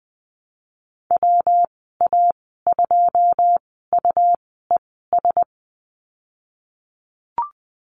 Morse code, heard as a keyed tone, about 20 wpm, spaced as for 15 wpm, 700 Hz. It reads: WA2UES
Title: 154 Lesson 25g - Letter p - Call Signs 20wpm